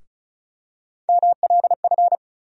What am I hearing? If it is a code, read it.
Morse code, heard as a keyed tone, about 36 wpm, 700 Hz. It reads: MLF
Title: USA Call Sign Postfixes - 3 Letters 35wpm